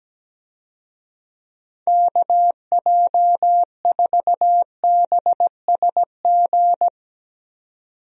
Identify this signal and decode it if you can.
Morse code, heard as a keyed tone, about 17 wpm, 700 Hz. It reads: KJ4BSG
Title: USA Call Signs - 6 Characters - 17wpm